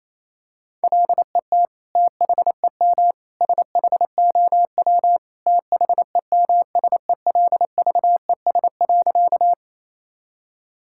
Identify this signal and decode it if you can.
Morse code, heard as a keyed tone, about 28 wpm, 700 Hz. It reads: LET THEM SHOW THEMSELVES.
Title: Sentences from Top 1000 Words - Review Part 5 - 28wpm